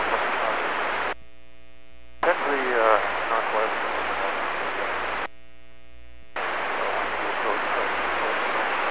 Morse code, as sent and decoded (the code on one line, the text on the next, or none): none